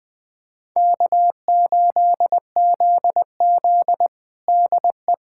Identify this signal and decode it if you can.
Morse code, heard as a keyed tone, about 20 wpm, 700 Hz. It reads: K8ZZ DE
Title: Practice QSO - Exchange 4 - Ending - 20wpm